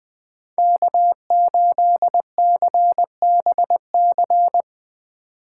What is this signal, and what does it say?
Morse code, heard as a keyed tone, about 23 wpm, 700 Hz. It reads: K8CBC
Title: Speed Racing - USA Call Signs 20wpm